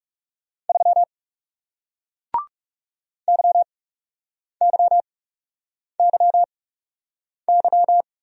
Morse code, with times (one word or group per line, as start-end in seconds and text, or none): 0.7-1.0 3
3.3-8.0 YYYY